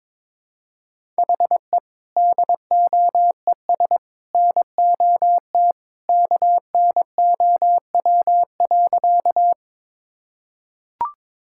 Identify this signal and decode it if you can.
Morse code, heard as a keyed tone, about 21 wpm, 700 Hz. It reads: HE DOES NOT KNOW.
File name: practice / 22wpm / Sentences with 3-Word Phrases - 22wpm